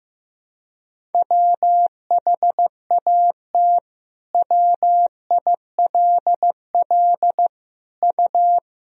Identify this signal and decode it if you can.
Morse code, heard as a keyed tone, about 15 wpm, 700 Hz. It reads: WHAT WILL U
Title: Sentences from Top 200 Words and Abbreviations 15wpm